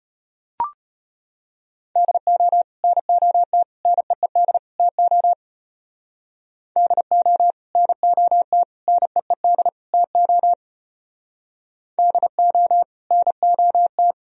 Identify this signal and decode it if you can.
Morse code, heard as a keyed tone, about 31 wpm, 700 Hz. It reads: DO NOT NEED TO DO NOT NEED TO DO NOT
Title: Speed-Racing - Sets of 4 Words 25wpm